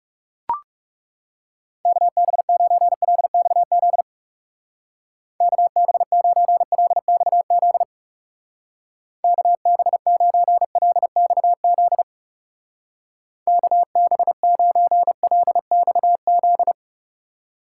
Morse code, heard as a keyed tone, about 37 wpm, 700 Hz. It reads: KB9LXZ KB9LXZ KB9LXZ KB9LXZ
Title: Speed Racing - USA Call Signs 30wpm